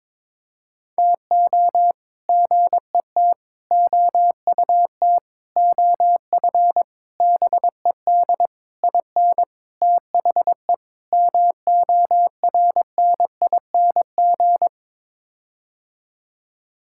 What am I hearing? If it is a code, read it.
Morse code, heard as a keyed tone, about 21 wpm, 700 Hz. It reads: TO GET OUT OF BED IN THE MORNING
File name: practice / 22wpm / Sets of 8 Words - 22wpm